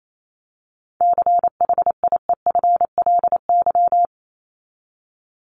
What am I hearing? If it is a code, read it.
Morse code, heard as a keyed tone, about 28 wpm, 700 Hz. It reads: CHIEFLY
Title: LY Suffix - Top 100 Words with Base Words 28wpm